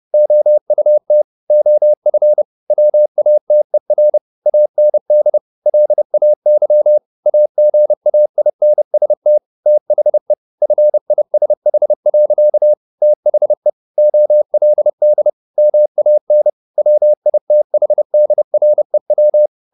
Morse code, heard as a keyed tone, about 30 wpm, 600 Hz. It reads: OUT OF WATER AND LAY AGAINST THE FISH. THE OLD MAN WITHDREW